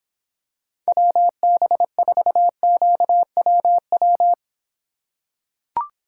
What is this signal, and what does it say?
Morse code, heard as a keyed tone, about 26 wpm, 700 Hz. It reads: WB4QWW